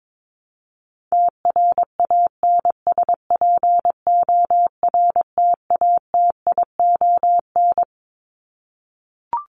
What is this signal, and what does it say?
Morse code, heard as a keyed tone, about 22 wpm, 700 Hz. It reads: TRANSPORTATION
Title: Top 1501-1750 Words 22wpm